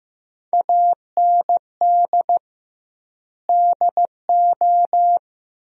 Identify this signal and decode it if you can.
Morse code, heard as a keyed tone, about 16 wpm, 700 Hz. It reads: AND DO
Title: 2x Word-Spacing - Sentences from Top 100 Words - 15wpm